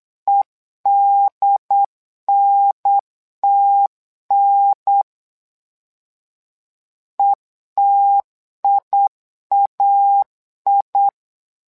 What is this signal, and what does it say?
Morse code, heard as a keyed tone, about 8 wpm, 800 Hz. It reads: EDNTN ETIAI
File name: LEZIONI / 2 UHDN / 2 UHDN 8wpm2232